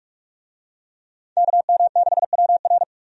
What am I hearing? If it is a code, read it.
Morse code, heard as a keyed tone, about 46 wpm, 700 Hz. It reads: KM6WR
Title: USA Call Signs - 5 Characters - 45wpm